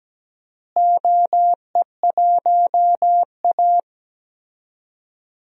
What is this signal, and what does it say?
Morse code, heard as a keyed tone, about 17 wpm, 700 Hz. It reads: OE1A
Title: DXCC-100 Call Signs - 4 Characters - Repeated 2x - 17wpm